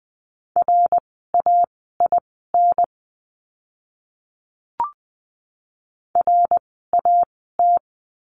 Morse code, heard as a keyed tone, about 20 wpm, spaced as for 15 wpm, 700 Hz. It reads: RAIN RAT
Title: 054 Lesson 10f - Letter r - Words 20wpm